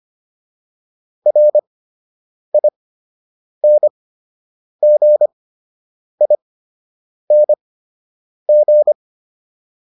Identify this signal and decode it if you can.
Morse code, heard as a keyed tone, about 25 wpm, spaced as for 8 wpm, 600 Hz. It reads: RINGING